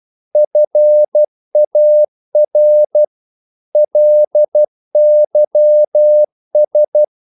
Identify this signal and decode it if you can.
Morse code, heard as a keyed tone, about 12 wpm, 600 Hz. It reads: FAR LYS
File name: Korte ord DK 12wpm